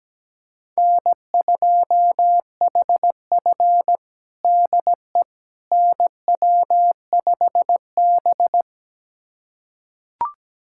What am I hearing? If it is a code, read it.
Morse code, heard as a keyed tone, about 17 wpm, 700 Hz. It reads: N2HF DE NW5B
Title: USA Callsign DE Callsign - 4 Characters - 17wpm